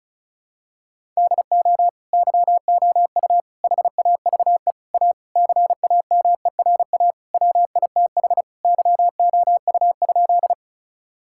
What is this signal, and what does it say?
Morse code, heard as a keyed tone, about 34 wpm, 700 Hz. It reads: DO YOU HAVE A CAMERA WITH YOU?
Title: Sentences from Top 1000 Words - Review Part 5 - 35wpm